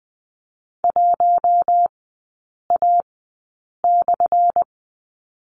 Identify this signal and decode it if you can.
Morse code, heard as a keyed tone, about 20 wpm, spaced as for 8 wpm, 700 Hz. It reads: 1A/
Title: 178 Lesson 29e - Slash - Three Characters 20wpm